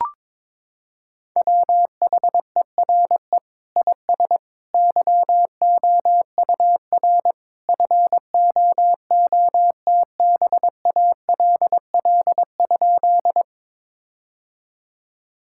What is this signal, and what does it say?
Morse code, heard as a keyed tone, about 21 wpm, 700 Hz. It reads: WHERE IS YOUR FOOTBALL?
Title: Sentences from Top 1600 Words 22wpm